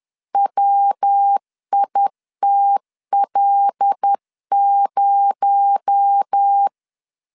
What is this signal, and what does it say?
Morse code, heard as a keyed tone, about 10 wpm, 800 Hz. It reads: WITL0
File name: LEZIONI / 7 GF9Z / 7 G 10wpm 2501